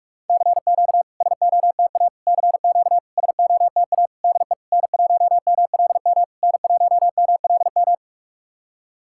Morse code, heard as a keyed tone, about 44 wpm, 690 Hz. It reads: CQ SOTA CQ SOTA DE N1MLM N1MLM
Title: SOTA QSO Practice 45wpm